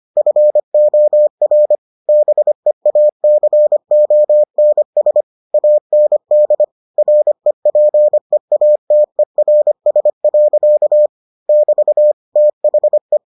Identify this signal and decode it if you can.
Morse code, heard as a keyed tone, about 25 wpm, 600 Hz. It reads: FOR BEACONS AND REPEATERS. <BT> THE